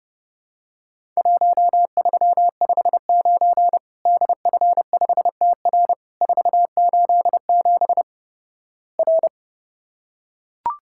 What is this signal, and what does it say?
Morse code, heard as a keyed tone, about 31 wpm, 700 Hz. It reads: 1359 DF5TR 487 R
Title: Worked All Europe DX Contest - QTC Exchange - 30wpm